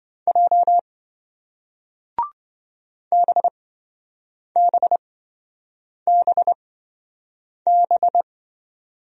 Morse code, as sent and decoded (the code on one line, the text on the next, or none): .--- / -... -... -... -...
J BBBB